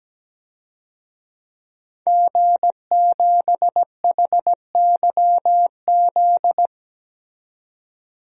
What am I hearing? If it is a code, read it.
Morse code, heard as a keyed tone, about 17 wpm, 700 Hz. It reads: G7HYZ
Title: DXCC-100 Call Signs - 5 Characters - 17wpm